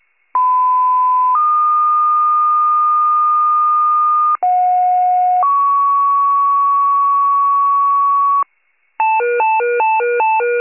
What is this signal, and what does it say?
Two-tone page, heard as a one-second tone, then a three-second tone, about 1010 then 1220 Hz, about 730 then 1120 Hz. It is heard from 0.3 to 8.4 s.
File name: wcalert